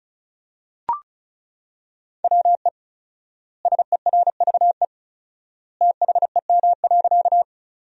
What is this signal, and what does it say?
Morse code, heard as a keyed tone, about 36 wpm, 700 Hz. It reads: WE SERVE THEM.
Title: 4x Word-Spacing - Sentences from Top 400 Words - 35wpm